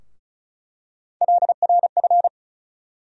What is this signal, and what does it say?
Morse code, heard as a keyed tone, about 34 wpm, 700 Hz. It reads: LRF
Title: USA Call Sign Postfixes - 3 Letters 35wpm